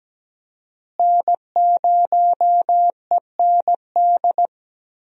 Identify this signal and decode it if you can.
Morse code, heard as a keyed tone, about 17 wpm, 700 Hz. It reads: N0END
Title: USA Call Signs - 5 Characters - Repeated 3x - 17wpm